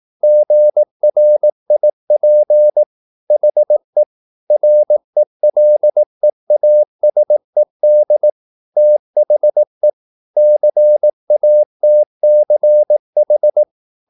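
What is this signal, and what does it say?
Morse code, heard as a keyed tone, about 18 wpm, 600 Hz. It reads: GRIP HE RELEASED THE CATCH
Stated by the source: Morse practice at 18 wpm, English text, from CW med Gnister.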